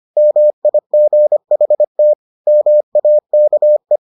Morse code, heard as a keyed tone, about 25 wpm, 600 Hz. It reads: MIGHT MAKE